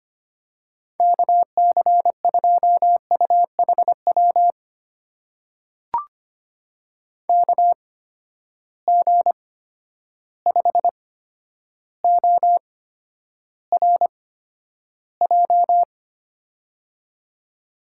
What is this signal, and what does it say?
Morse code, heard as a keyed tone, about 25 wpm, spaced as for 11 wpm, 700 Hz. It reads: KC2UHW KG5ORJ
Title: USA Call Signs - 6 Characters - 8x Character Spacing - 25wpm